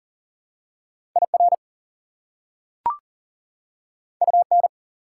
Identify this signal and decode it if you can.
Morse code, heard as a keyed tone, about 37 wpm, 700 Hz. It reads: IR UN
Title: Prefixes and Suffixes 40wpm